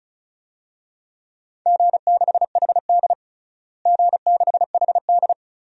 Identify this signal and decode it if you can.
Morse code, heard as a keyed tone, about 34 wpm, 700 Hz. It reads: G6HD G6HD
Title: DXCC-100 Call Signs - 4 Characters - Repeated 2x - 35wpm